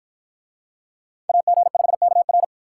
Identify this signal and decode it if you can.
Morse code, heard as a keyed tone, about 53 wpm, 700 Hz. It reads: AG5KR